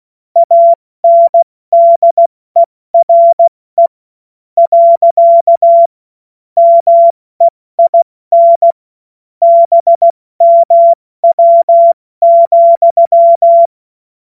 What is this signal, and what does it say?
Morse code, heard as a keyed tone, about 16 wpm, spaced as for 13 wpm, 680 Hz. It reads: ANDERE. MEIN BMW,
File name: Morsekurs / Witze / Witz 17. Mai 2022 70 F80